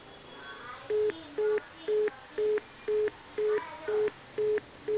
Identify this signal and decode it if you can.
Telephone call progress tone (425 Hz), heard as a telephone network's tone, about 420 Hz. Congestion tone.